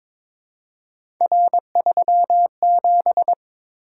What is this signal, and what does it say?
Morse code, heard as a keyed tone, about 22 wpm, 700 Hz. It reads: R37
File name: practice / 22wpm / Three Letter-Number Combinations - 22wpm